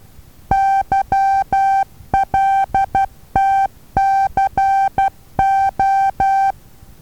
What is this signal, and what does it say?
Morse code, heard as a keyed tone, about 12 wpm, 780 Hz. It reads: YLTCO
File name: Taviro HA6VA B